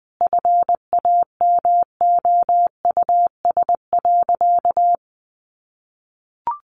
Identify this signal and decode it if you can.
Morse code, heard as a keyed tone, about 20 wpm, 700 Hz. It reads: FAMOUS.
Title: Sentences from Top 1500 Words 20wpm